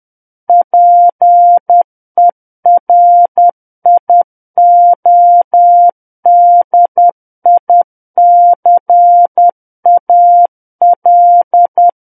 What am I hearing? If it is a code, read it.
Morse code, heard as a keyed tone, about 10 wpm, 700 Hz. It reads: PERIODICAL
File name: Book One / 01-Chapter 1 - 10 WPM